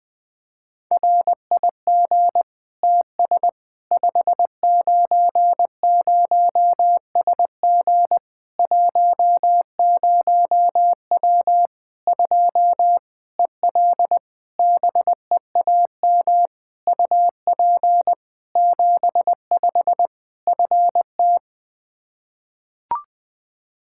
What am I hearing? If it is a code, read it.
Morse code, heard as a keyed tone, about 20 wpm, 700 Hz. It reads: RIG TS 590SG 10W 2 EL BEAM UP 75 FT
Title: Practice QSO - Exchange 2 - Info - 20wpm